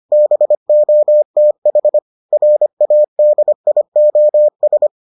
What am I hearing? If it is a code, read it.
Morse code, heard as a keyed tone, about 25 wpm, 600 Hz. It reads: BOTH RADIOS